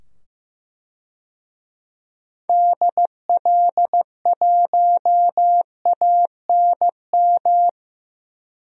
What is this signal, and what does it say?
Morse code, heard as a keyed tone, about 15 wpm, 700 Hz. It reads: DL1ANM
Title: DXCC-100 Call Signs - 6 Characters - 15wpm